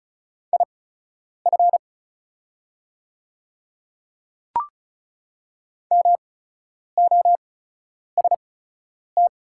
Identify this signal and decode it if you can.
Morse code, heard as a keyed tone, about 36 wpm, spaced as for 10 wpm, 690 Hz. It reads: IF MOST
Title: 8x Character Spacing - Top 100 Words - 35wpm